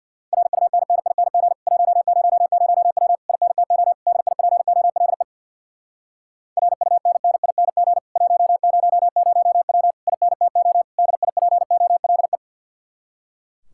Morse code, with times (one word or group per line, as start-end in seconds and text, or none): 0.3-1.5 RUNNING
1.7-3.2 100W
3.3-3.9 INTO
4.1-5.2 DIPOLE
6.6-8.0 RUNNING
8.2-9.9 100W
10.1-10.8 INTO
11.0-12.4 DIPOLE